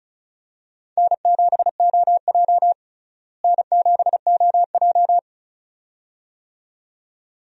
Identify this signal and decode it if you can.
Morse code, heard as a keyed tone, about 36 wpm, 700 Hz. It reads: N7OJ N7OJ
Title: USA Call Signs - 4 Characters - Repeated 2x - 35wpm